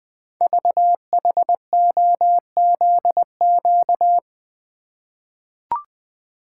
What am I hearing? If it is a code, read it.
Morse code, heard as a keyed tone, about 20 wpm, 700 Hz. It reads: VHOZQ
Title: Code Groups - 5 Letters - 20wpm